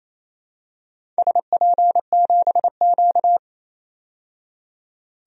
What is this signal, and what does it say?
Morse code, heard as a keyed tone, about 28 wpm, 700 Hz. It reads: SP7Q
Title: DXCC-100 Call Signs - 4 Characters - 28wpm